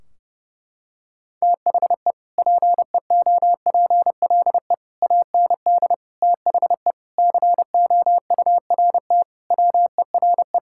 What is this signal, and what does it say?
Morse code, heard as a keyed tone, about 30 wpm, 700 Hz. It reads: THE PEOPLE AND THE COURT WERE
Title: Sentences from Top 1600 Words 30wpm